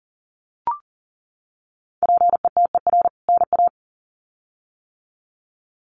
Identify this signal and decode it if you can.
Morse code, heard as a keyed tone, about 40 wpm, 700 Hz. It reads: PETER NA
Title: CWT Exchange 40wpm